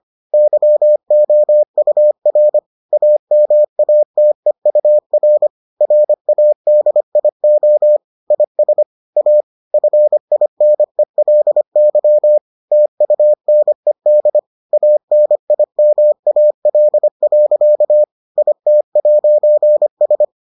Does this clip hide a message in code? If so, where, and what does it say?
Yes: Morse code from 0.3 to 20.2 s, heard as a keyed tone, about 25 wpm, 600 Hz. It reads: YOUR AMATEUR RADIO IS A FINELY TUNED ANIMAL. IT'S